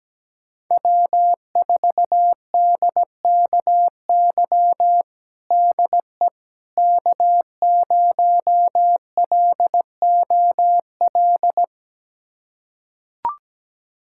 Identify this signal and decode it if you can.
Morse code, heard as a keyed tone, about 17 wpm, 700 Hz. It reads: W4DKY DE K0LOL
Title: USA Callsign DE Callsign - 5 Characters - 17wpm